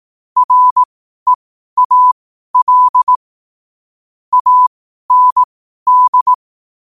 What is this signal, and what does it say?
Morse code, heard as a keyed tone, about 18 wpm, spaced as for 12 wpm, 1000 Hz. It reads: REAL AND
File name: qotdfiles / 12wpm